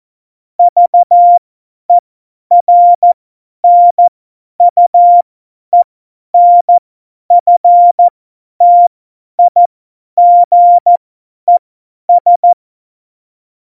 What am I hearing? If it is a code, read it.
Morse code, heard as a keyed tone, about 14 wpm, spaced as for 10 wpm, 700 Hz. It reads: VERNUENFTIGES
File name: Morsekurs / Witze / Witz 16. April 2019, 50 F70